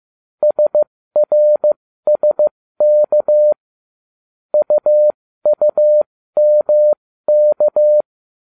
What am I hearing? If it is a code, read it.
Morse code, heard as a keyed tone, about 15 wpm, spaced as for 13 wpm, 600 Hz. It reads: SRSK UUMK